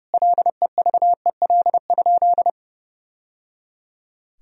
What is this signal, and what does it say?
Morse code, heard as a keyed tone, about 30 wpm, 700 Hz. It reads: LEVEL?
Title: Sentences from Top 1000 Words - Review Part 4 - 30wpm